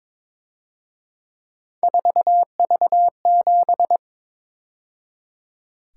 Morse code, heard as a keyed tone, about 26 wpm, 700 Hz. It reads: 4V7
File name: Speed-Racing - Three Letter-Number Combinations - 22wpm